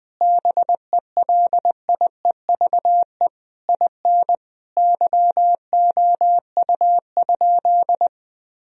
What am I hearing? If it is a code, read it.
Morse code, heard as a keyed tone, about 20 wpm, 700 Hz. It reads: BELIEVE IN YOU?
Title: Sentences from Top 1000 Words - Review Part 3 - 20wpm